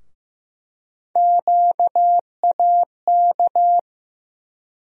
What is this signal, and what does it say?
Morse code, heard as a keyed tone, about 15 wpm, 700 Hz. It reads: QAK
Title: USA Call Sign Postfixes - 3 Letters 15wpm